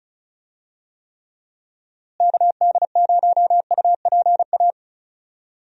Morse code, heard as a keyed tone, about 36 wpm, 700 Hz. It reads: KD0UPA
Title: USA Call Signs - 6 Characters - Repeated 2x - 35wpm